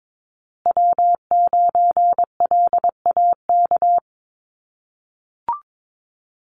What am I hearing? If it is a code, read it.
Morse code, heard as a keyed tone, about 22 wpm, 700 Hz. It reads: W9LAK
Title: USA Call Signs Repeated 3x 22wpm